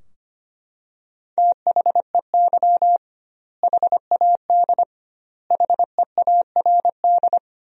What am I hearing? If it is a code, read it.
Morse code, heard as a keyed tone, about 25 wpm, 700 Hz. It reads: THEY HAD HEARD
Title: 2x Word-Spacing - Sentences from Top 400 Words - 25wpm